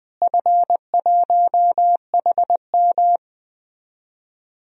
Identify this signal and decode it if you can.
Morse code, heard as a keyed tone, about 20 wpm, 700 Hz. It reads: F1HM
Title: DXCC-100 Call Signs - 4 Characters - Repeated 2x - 20wpm